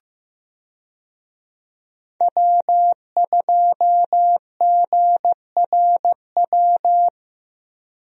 Morse code, heard as a keyed tone, about 15 wpm, 700 Hz. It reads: W2GRW
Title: USA Call Signs - 5 Characters - Repeated 3x - 15wpm